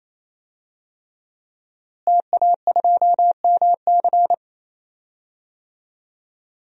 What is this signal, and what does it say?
Morse code, heard as a keyed tone, about 28 wpm, 700 Hz. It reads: TA2MC
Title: DXCC-100 Call Signs - 5 Characters - Repeated 3x - 28wpm